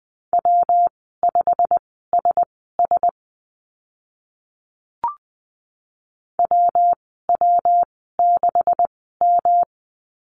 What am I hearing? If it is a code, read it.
Morse code, heard as a keyed tone, about 20 wpm, spaced as for 15 wpm, 700 Hz. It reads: W5SS WW6M
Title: 128 Lesson 21g - Number 6 - Call Signs 20wpm